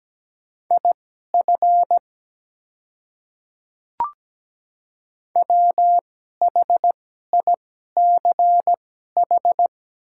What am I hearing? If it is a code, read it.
Morse code, heard as a keyed tone, about 17 wpm, spaced as for 12 wpm, 700 Hz. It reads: IF WHICH
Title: 2x Character Spacing - Top 100 Words - 17wpm